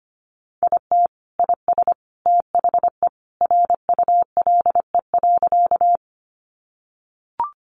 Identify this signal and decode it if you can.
Morse code, heard as a keyed tone, about 25 wpm, 700 Hz. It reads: IT IS THE RULE.
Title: Sentences from Top 400 Words 25wpm